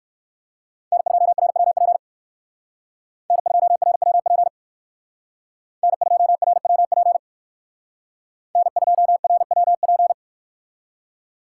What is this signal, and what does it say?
Morse code, heard as a keyed tone, about 56 wpm, 700 Hz. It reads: N2RWP N2RWP N2RWP N2RWP
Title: Speed Racing - USA 1x3 Call Signs 45wpm